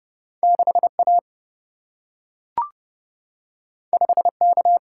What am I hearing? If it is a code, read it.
Morse code, heard as a keyed tone, about 30 wpm, 700 Hz. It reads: 6A 5K
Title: Single-Letter Single-Number 30wpm